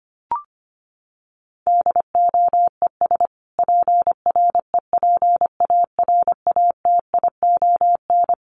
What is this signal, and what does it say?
Morse code, heard as a keyed tone, about 25 wpm, 700 Hz. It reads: DOES PREPARATION
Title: Sentences from Top 1400 Words 25wpm